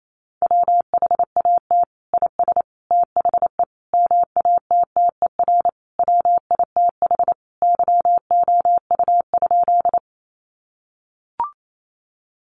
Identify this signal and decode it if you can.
Morse code, heard as a keyed tone, about 28 wpm, 700 Hz. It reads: WHAT IS THE MATTER WITH YOU?
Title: Sentences from Top 700 Words 28wpm